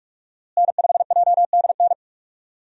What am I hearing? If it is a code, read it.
Morse code, heard as a keyed tone, about 43 wpm, 700 Hz. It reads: N5JDN
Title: USA Call Signs - 5 Characters - Repeated 3x - 45wpm